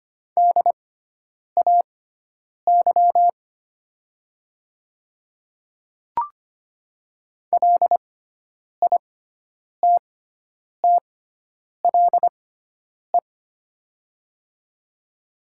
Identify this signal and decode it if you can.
Morse code, heard as a keyed tone, about 25 wpm, spaced as for 9 wpm, 700 Hz. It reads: DAY LITTLE
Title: 6x Character Spacing - Top 300 Words - 25wpm